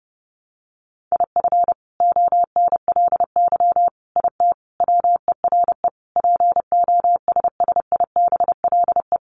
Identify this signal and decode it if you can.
Morse code, heard as a keyed tone, about 30 wpm, 700 Hz. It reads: IF ONLY IT WERE POSSIBLE
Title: Sentences from Top 500 Words 30wpm